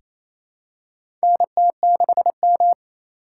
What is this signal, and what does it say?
Morse code, heard as a keyed tone, about 28 wpm, 700 Hz. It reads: NT6M
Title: USA Call Signs - 3x Character Spacing - 28wpm